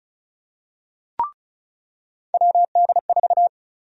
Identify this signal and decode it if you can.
Morse code, heard as a keyed tone, about 34 wpm, 700 Hz. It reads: WD4
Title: USA Call Sign Prefixes - 35wpm